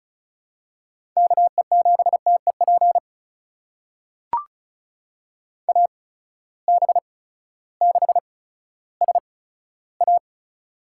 Morse code, heard as a keyed tone, about 36 wpm, spaced as for 16 wpm, 700 Hz. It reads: KE7TEP AB6SA
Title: USA Call Signs - 8x Character Spacing - 35wpm